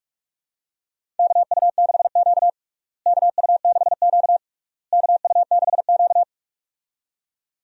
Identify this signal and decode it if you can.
Morse code, heard as a keyed tone, about 46 wpm, 700 Hz. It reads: KU6Q KU6Q KU6Q
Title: USA Call Signs - 4 Characters - Repeated 3x - 45wpm